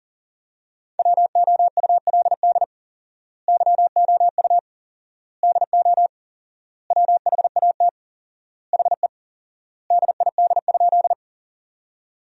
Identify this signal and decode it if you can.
Morse code, heard as a keyed tone, about 40 wpm, 700 Hz. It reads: WOULD YOU DO WHAT HE DID?
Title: 4x Word-Spacing - Sentences from Top 100 Words - 40wpm